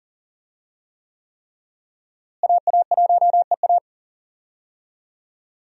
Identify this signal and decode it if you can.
Morse code, heard as a keyed tone, about 40 wpm, 700 Hz. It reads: AA1EA
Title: USA Call Signs - 5 Characters - Repeated 3x - 40wpm